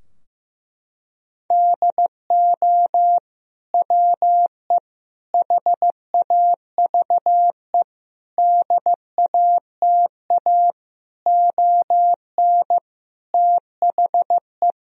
Morse code, heard as a keyed tone, about 15 wpm, 700 Hz. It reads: DO WE HAVE DATA ON THE
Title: Sentences from Top 1000 Words - Review Part 3 - 15wpm